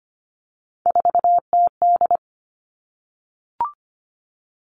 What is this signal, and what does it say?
Morse code, heard as a keyed tone, about 25 wpm, 700 Hz. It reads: 4TD
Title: All Cut Numbers - 3-Digit 25wpm